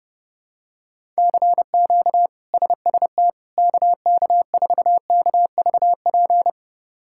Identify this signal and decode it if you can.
Morse code, heard as a keyed tone, about 30 wpm, 700 Hz. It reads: CQ SST KK4KVP
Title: K1USN SST QSO Practice - 30wpm